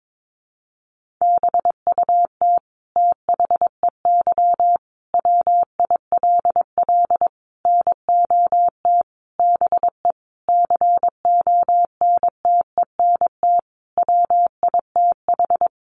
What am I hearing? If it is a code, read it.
Morse code, heard as a keyed tone, about 22 wpm, 700 Hz. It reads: BUT THEY WILL NOT BE CONTENT WITH
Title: Sentences from Top 900 Words 22wpm